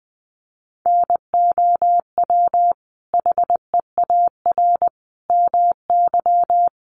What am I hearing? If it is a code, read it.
Morse code, heard as a keyed tone, about 20 wpm, 700 Hz. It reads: NOW HEAR MY
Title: Sentences from Top 1100 Words 20wpm